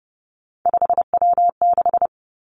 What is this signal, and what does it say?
Morse code, heard as a keyed tone, about 30 wpm, 700 Hz. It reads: 5W6
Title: Three Challenging Letter-Number Combinations 30wpm